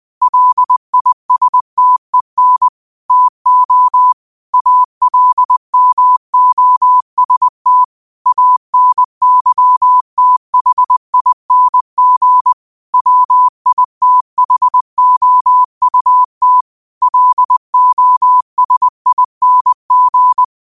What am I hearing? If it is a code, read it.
Morse code, heard as a keyed tone, about 20 wpm, 1000 Hz. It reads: LISTEN TO ALMOST ANYTHING WITHOUT LOSING
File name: qotdfiles / 20wpm